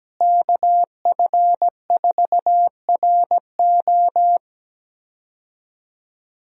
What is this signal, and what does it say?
Morse code, heard as a keyed tone, about 17 wpm, 700 Hz. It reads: KF4RO
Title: USA Call Signs - 5 Characters - Repeated 2x - 17wpm